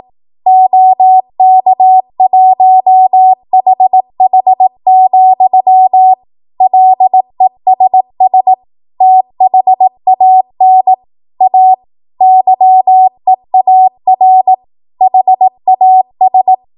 Morse code, heard as a keyed tone, about 18 wpm, 750 Hz. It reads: OK1HH, LESS THAN A YEAR HAS
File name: Morse / 18WPM